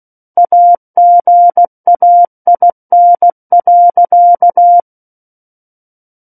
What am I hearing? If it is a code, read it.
Morse code, heard as a keyed tone, about 16 wpm, 700 Hz. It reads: AGAIN.
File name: Book One / 07-Chapter 7 - 16 WPM